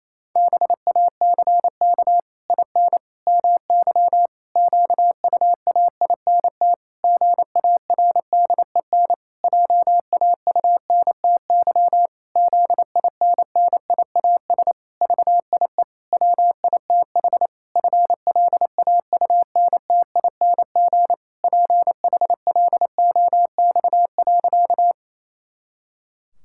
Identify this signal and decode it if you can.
Morse code, heard as a keyed tone, about 28 wpm, 700 Hz. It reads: BACK IN MY QUAINT GARDEN JAUNTY ZINNIAS VIE WITH FLAUNTING PHLOX.
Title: Pangram Challenge v2 - 28wpm